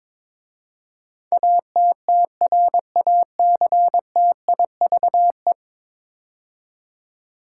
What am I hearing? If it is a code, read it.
Morse code, heard as a keyed tone, about 21 wpm, 700 Hz. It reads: ATTRACTIVE
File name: practice / 22wpm / 10 Letter Words - First 3 Letters Given - 22wpm